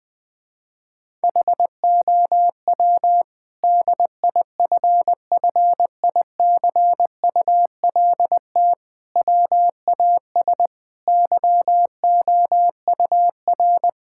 Morse code, heard as a keyed tone, about 20 wpm, 700 Hz. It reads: HOW DIFFICULT WAS YOUR
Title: Sentences from Top 1000 Words - Review Part 6 - 20wpm